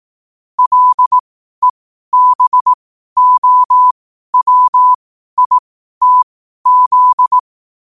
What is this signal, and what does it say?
Morse code, heard as a keyed tone, about 18 wpm, spaced as for 12 wpm, 1000 Hz. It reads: LEBOWITZ
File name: qotdfiles / 12wpm